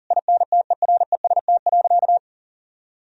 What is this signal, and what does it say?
Morse code, heard as a keyed tone, about 40 wpm, 700 Hz. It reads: INTEREST.
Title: Sentences from Top 1000 Words - Review Part 4 - 40wpm